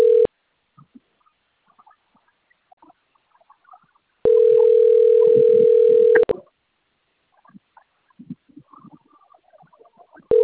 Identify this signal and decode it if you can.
Ringback tone (North America), 440 + 480 Hz.